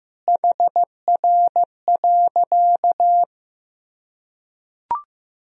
Morse code, heard as a keyed tone, about 15 wpm, 700 Hz. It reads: HR.
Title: Sentences from Top 200 Words and Abbreviations 15wpm